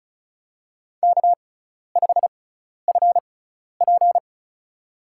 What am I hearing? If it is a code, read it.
Morse code, heard as a keyed tone, about 36 wpm, spaced as for 12 wpm, 700 Hz. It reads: K5FP
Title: USA Call Signs - 4 Characters - 6x Character Spacing - 35wpm